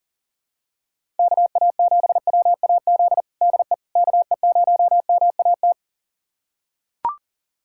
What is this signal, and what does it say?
Morse code, heard as a keyed tone, about 40 wpm, 700 Hz. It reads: KA7WAZ DE KE0MAT
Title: USA Callsign DE Callsign - 6 Characters - 40wpm